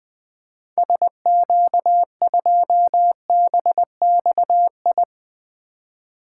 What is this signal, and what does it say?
Morse code, heard as a keyed tone, about 21 wpm, 700 Hz. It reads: SQ2BXI
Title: 2023 POTA Activator Call Signs - 20wpm